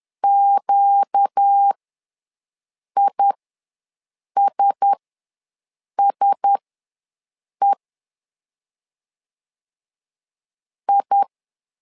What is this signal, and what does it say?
Morse code, heard as a keyed tone, about 10 wpm, spaced as for 6 wpm, 800 Hz. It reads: QISSE I
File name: LEZIONI / 4 Q1S ws 10wpm 2208